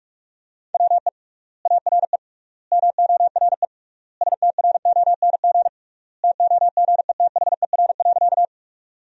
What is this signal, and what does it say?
Morse code, heard as a keyed tone, about 46 wpm, 700 Hz. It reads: WE ARE MORE STRONG TOGETHER.
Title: 3x Word-Spacing - Sentences from Top 500 Words - 45wpm